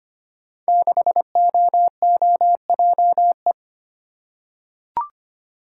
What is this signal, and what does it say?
Morse code, heard as a keyed tone, about 25 wpm, 700 Hz. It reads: 6OOJE
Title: Code Groups - 5 Letter:Numbers - 25wpm